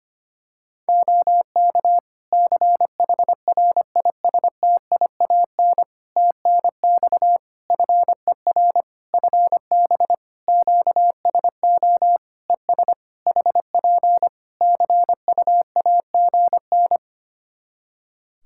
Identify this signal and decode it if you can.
Morse code, heard as a keyed tone, about 25 wpm, 700 Hz. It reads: OK CHRISTIAN TNX FER FB QSO ES HP CUAGN
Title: Practice QSO - Exchange 4 - Ending - 25wpm